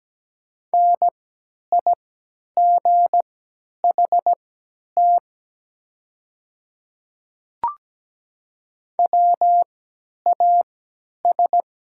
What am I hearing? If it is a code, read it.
Morse code, heard as a keyed tone, about 17 wpm, spaced as for 10 wpm, 690 Hz. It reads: NIGHT WAS
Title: 3x Character Spacing - Top 300 Words - 17wpm